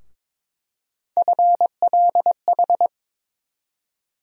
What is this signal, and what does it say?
Morse code, heard as a keyed tone, about 21 wpm, 700 Hz. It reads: FLH
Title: USA Call Sign Postfixes - 3 Letters 22wpm